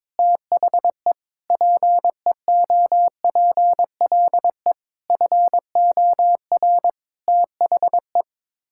Morse code, heard as a keyed tone, about 22 wpm, 700 Hz. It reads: THE PEOPLE FOR THE
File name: practice / 22wpm / Sets of 8 Words - 22wpm